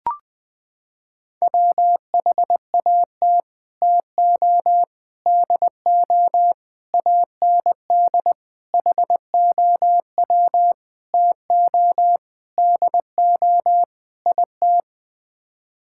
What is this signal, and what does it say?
Morse code, heard as a keyed tone, about 20 wpm, 700 Hz. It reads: WHAT TO DO AND HOW TO DO IT
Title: Sets of 8 Words - 20wpm